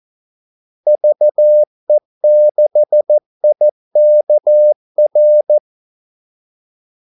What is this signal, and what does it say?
Morse code, heard as a keyed tone, about 14 wpm, 600 Hz. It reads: VE6IKR
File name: Callsigns-14wpm